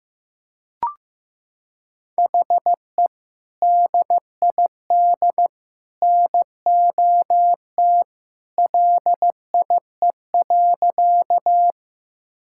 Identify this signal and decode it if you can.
Morse code, heard as a keyed tone, about 16 wpm, 700 Hz. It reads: HE DID NOT LIE.
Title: Sentences from Top 1600 Words 15wpm